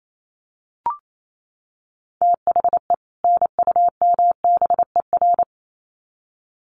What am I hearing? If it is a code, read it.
Morse code, heard as a keyed tone, about 28 wpm, 700 Hz. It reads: THE NUMBER
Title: Sets of 2 Words 28wpm